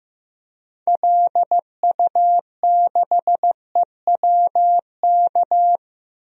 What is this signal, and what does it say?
Morse code, heard as a keyed tone, about 15 wpm, 700 Hz. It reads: LU6EWK
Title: DXCC-100 Call Signs - 6 Characters - 15wpm